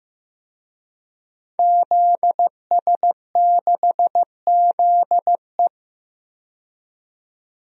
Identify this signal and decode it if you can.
Morse code, heard as a keyed tone, about 15 wpm, 700 Hz. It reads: ZS6ZE
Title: DXCC-100 Call Signs - 5 Characters - 15wpm